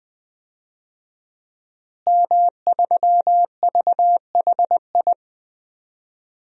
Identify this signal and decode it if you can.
Morse code, heard as a keyed tone, about 20 wpm, 700 Hz. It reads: M3VHI
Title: DXCC-100 Call Signs - 5 Characters - Repeated 2x - 20wpm